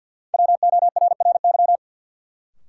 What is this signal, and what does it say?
Morse code, heard as a keyed tone, about 53 wpm, 700 Hz. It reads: WORRY
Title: Sets of 5 Words - New Word Drill - 50wpm